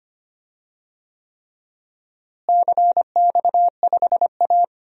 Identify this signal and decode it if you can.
Morse code, heard as a keyed tone, about 25 wpm, 700 Hz. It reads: CX5A
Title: DXCC-100 Call Signs - 4 Characters - Repeated 3x - 25wpm